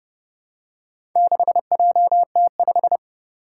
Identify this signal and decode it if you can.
Morse code, heard as a keyed tone, about 30 wpm, 690 Hz. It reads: 6JT5
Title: Four Challenging Letter-Number Combinations 30wpm